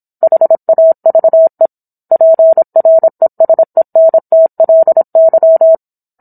Morse code, heard as a keyed tone, about 27 wpm, 650 Hz. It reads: HAVE PRESENTLY